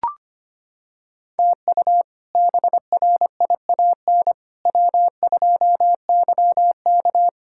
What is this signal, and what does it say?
Morse code, heard as a keyed tone, about 25 wpm, 700 Hz. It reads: TU BRIAN W2YK
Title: K1USN SST QSO Practice - 25wpm